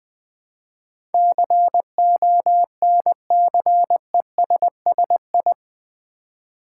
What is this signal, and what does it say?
Morse code, heard as a keyed tone, about 20 wpm, 700 Hz. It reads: CONCESSI
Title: Build A Word - 4 Letters At A Time - 10-Letter Words - 20wpm